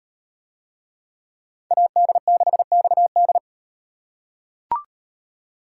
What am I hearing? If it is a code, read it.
Morse code, heard as a keyed tone, about 35 wpm, 700 Hz. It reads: AD6XD